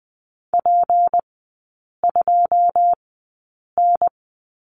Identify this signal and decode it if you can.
Morse code, heard as a keyed tone, about 20 wpm, spaced as for 8 wpm, 700 Hz. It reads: P2N
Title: 152 Lesson 25e - Letter p - Three Characters 20wpm